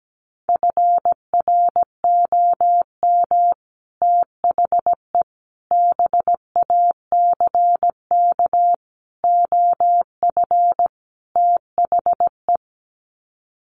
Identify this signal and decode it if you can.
Morse code, heard as a keyed tone, about 17 wpm, 700 Hz. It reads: FROM THE BACK OF THE
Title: Sets of 5 Words Encore 17wpm